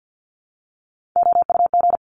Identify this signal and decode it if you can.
Morse code, heard as a keyed tone, about 50 wpm, 700 Hz. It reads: OUG
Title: Top Three Letter Combinations in Words 50wpm